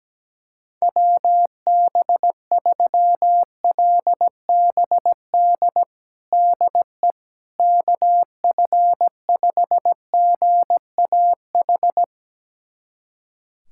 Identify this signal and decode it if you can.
Morse code, heard as a keyed tone, about 17 wpm, 700 Hz. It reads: WB3LBD DE KF5GAH
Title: USA Callsign DE Callsign 17wpm